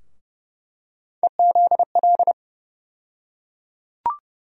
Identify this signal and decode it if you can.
Morse code, heard as a keyed tone, about 30 wpm, 700 Hz. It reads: EZL
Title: USA Call Sign Postfixes - 3 Letters 30wpm